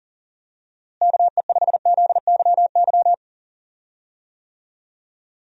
Morse code, heard as a keyed tone, about 40 wpm, 700 Hz. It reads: KE5ZYY
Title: USA Call Signs - 6 Characters - Repeated 2x - 40wpm